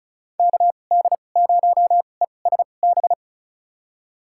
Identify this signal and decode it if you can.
Morse code, heard as a keyed tone, about 34 wpm, spaced as for 25 wpm, 700 Hz. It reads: KD0ESB
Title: USA Call Signs - 2x Character Spacing - 35wpm